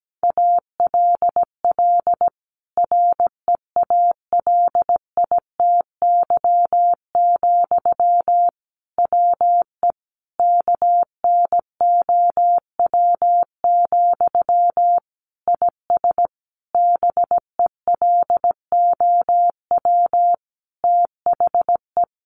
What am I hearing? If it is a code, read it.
Morse code, heard as a keyed tone, about 17 wpm, 700 Hz. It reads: ALL REALITY, WE KNOW, IS BELOW THE
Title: Sentences from Top 1400 Words 17wpm